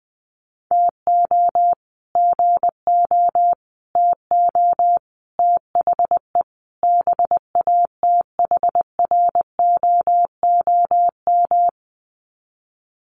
Morse code, heard as a keyed tone, about 20 wpm, 700 Hz. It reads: TO GO TO THE BATHROOM
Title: Sets of 5 Words Encore 20wpm